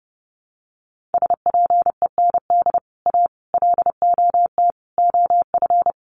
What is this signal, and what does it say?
Morse code, heard as a keyed tone, about 30 wpm, 700 Hz. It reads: SPEND A LOT OF
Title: Sets of 5 Words 30wpm